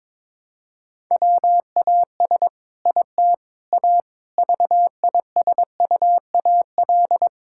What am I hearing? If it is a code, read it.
Morse code, heard as a keyed tone, about 22 wpm, 700 Hz. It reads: WAS IT A VISUAL
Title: Sentences from Top 1000 Words - Review Part 6 - 22wpm